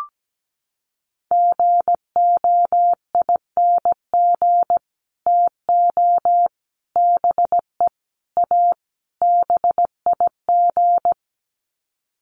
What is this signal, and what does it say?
Morse code, heard as a keyed tone, about 17 wpm, 700 Hz. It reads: GOING TO BE A BIG
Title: Sets of 5 Words Encore2 17wpm